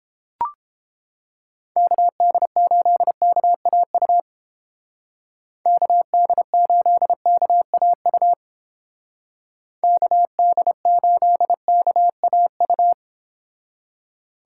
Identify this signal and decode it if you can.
Morse code, heard as a keyed tone, about 29 wpm, 700 Hz. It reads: KD8KAU KD8KAU KD8KAU